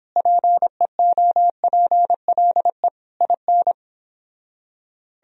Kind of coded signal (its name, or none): Morse code